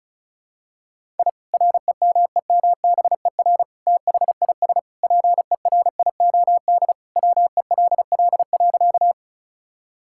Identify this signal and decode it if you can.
Morse code, heard as a keyed tone, about 36 wpm, 700 Hz. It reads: I REMEMBER THIS PERIOD WELL.
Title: Sentences from Top 1000 Words - Review Part 5 - 35wpm